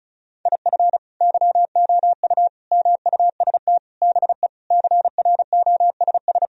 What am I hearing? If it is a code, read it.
Morse code, heard as a keyed tone, about 35 wpm, 700 Hz. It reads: IF YOU MUST BE CROSS
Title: Sentences from Top 1000 Words - Review Part 1 - 35wpm